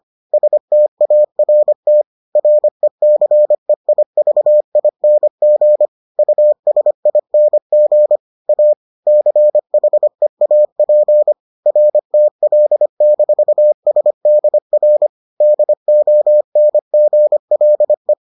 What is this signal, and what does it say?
Morse code, heard as a keyed tone, about 25 wpm, 600 Hz. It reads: START RECEIVING USING A CHEAP RTL-SDR DONGLE